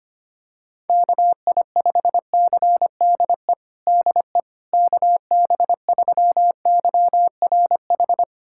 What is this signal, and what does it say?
Morse code, heard as a keyed tone, about 25 wpm, 700 Hz. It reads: KI5CDE DE KB3YRH
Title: USA Callsign DE Callsign 25wpm